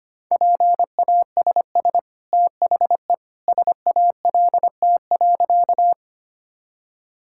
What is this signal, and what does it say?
Morse code, heard as a keyed tone, about 25 wpm, 700 Hz. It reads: PASS THE SALT.
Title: Sentences from Top 1000 Words - Review Part 5 - 25wpm